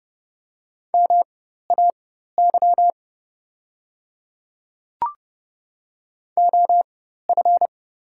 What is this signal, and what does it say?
Morse code, heard as a keyed tone, about 30 wpm, spaced as for 14 wpm, 700 Hz. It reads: MAY OF
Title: 4x Character Spacing - Top 100 Words - 30wpm